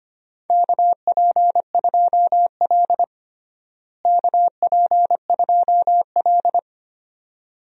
Morse code, heard as a keyed tone, about 25 wpm, 700 Hz. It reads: KP2L KP2L
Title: DXCC-100 Call Signs - 4 Characters - Repeated 2x - 25wpm